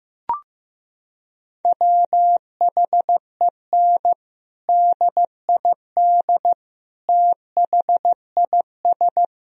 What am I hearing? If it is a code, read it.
Morse code, heard as a keyed tone, about 15 wpm, 700 Hz. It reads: WHEN DID THIS
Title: Sentences from Top 1000 Words - Review Part 1 - 15wpm